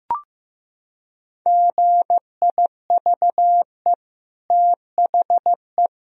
Morse code, heard as a keyed tone, about 15 wpm, 700 Hz. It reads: GIVE THE
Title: Sentences from Top 1000 Words - Review Part 2 - 15wpm